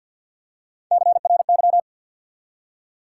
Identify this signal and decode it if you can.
Morse code, heard as a keyed tone, about 53 wpm, 700 Hz. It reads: CRY
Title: Three Letter Words - 50wpm